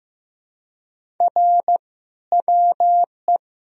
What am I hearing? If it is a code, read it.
Morse code, heard as a keyed tone, about 15 wpm, 700 Hz. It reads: R WE
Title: Sentences from Top 400 Words and Abbreviations 15wpm